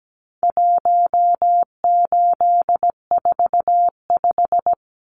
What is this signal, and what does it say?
Morse code, heard as a keyed tone, about 17 wpm, 700 Hz. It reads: 1845
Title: Numbers - 4-Digit 17wpm